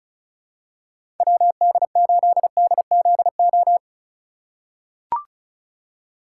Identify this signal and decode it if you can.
Morse code, heard as a keyed tone, about 36 wpm, 700 Hz. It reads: WD8DZO
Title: USA Call Signs - 2x Character Spacing - 35wpm